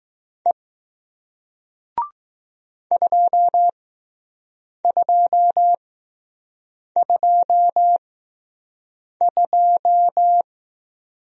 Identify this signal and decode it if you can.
Morse code, heard as a keyed tone, about 20 wpm, spaced as for 6 wpm, 700 Hz. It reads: E 2222